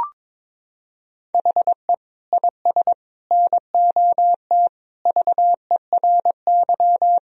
Morse code, heard as a keyed tone, about 22 wpm, 700 Hz. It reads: HE IS NOT VERY
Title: Sentences from Top 1000 Words - Review Part 1 - 22wpm